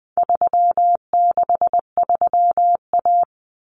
Morse code, heard as a keyed tone, about 20 wpm, 700 Hz. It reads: 363A
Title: Common Cut Numbers - 4-Digit 20wpm